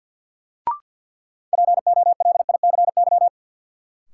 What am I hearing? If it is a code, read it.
Morse code, heard as a keyed tone, about 48 wpm, 700 Hz. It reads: POLICY
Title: Sentences from Top 1200 Words - New Word Drill Alt - 50wpm